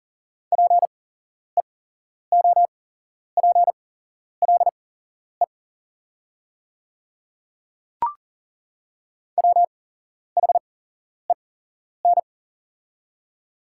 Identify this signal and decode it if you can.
Morse code, heard as a keyed tone, about 40 wpm, spaced as for 11 wpm, 690 Hz. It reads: PEOPLE WHEN
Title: 8x Character Spacing - Top 300 Words - 40wpm